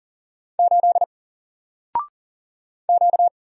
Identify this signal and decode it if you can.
Morse code, heard as a keyed tone, about 39 wpm, 700 Hz. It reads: 8 Q
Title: Single Character (Letter:Number:Punctuation:Prosign) 40wpm